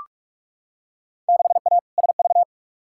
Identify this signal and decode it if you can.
Morse code, heard as a keyed tone, about 42 wpm, 700 Hz. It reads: 6A SV
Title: Field Day Exchanges Encore - 45wpm